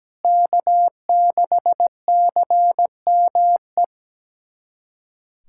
Morse code, heard as a keyed tone, about 17 wpm, 700 Hz. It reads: K6CME
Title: USA Callsign DE Callsign - 5 Characters - 17wpm